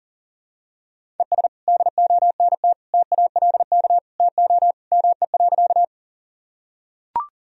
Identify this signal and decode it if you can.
Morse code, heard as a keyed tone, about 40 wpm, 700 Hz. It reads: ES DONT TALK TO ME.
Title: Sentences from Top 400 Words and Abbreviations 40wpm